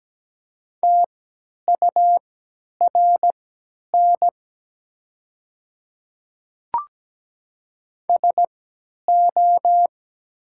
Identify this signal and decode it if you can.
Morse code, heard as a keyed tone, about 17 wpm, spaced as for 10 wpm, 700 Hz. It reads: TURN SO
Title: 3x Character Spacing - Top 200 Words - 17wpm